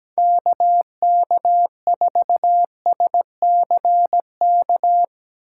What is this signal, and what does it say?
Morse code, heard as a keyed tone, about 17 wpm, 700 Hz. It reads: KK4SCK
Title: USA Call Signs - 3x Character Spacing - 17wpm